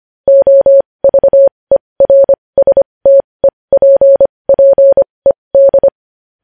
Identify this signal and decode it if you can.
Morse code, heard as a keyed tone, about 25 wpm, spaced as for 20 wpm, 550 Hz. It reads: OVERSTEPPED
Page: news-headlines-in-morse-code-at-20-wpm